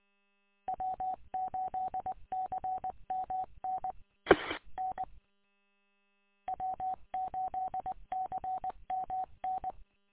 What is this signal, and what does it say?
Morse code, heard as a keyed tone, about 20 wpm, 750 Hz. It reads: W8CMN E N W8CMN
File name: w8cmn- -combined